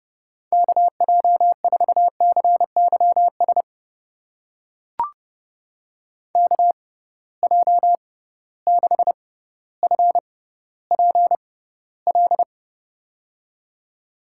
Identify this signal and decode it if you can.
Morse code, heard as a keyed tone, about 30 wpm, spaced as for 15 wpm, 700 Hz. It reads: KJ4CYS KJ6FPL
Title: USA Call Signs - 6 Characters - 6x Character Spacing - 30wpm